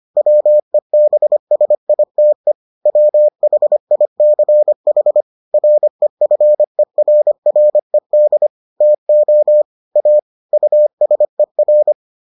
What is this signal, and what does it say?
Morse code, heard as a keyed tone, about 25 wpm, 600 Hz. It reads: WEBSITE WHICH REFERRED TO A USER